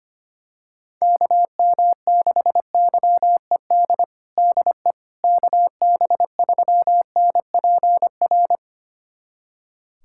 Morse code, heard as a keyed tone, about 25 wpm, 700 Hz. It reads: KM6YED DE KB3NPR
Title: USA Callsign DE Callsign - 6 Characters - 25wpm